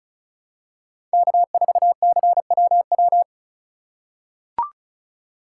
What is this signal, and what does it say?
Morse code, heard as a keyed tone, about 34 wpm, 700 Hz. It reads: K4CWW
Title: USA Call Signs - 5 Characters - 3x Character Spacing - 35wpm